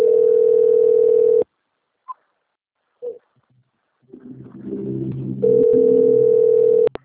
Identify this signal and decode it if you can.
Ringback tone (North America), 440 + 480 Hz.
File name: out- - - - - .28776